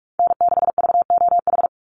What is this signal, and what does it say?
Morse code, heard as a keyed tone, about 44 wpm, 700 Hz. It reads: N6VOH
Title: Call Signs Repeated 3x - 45wpm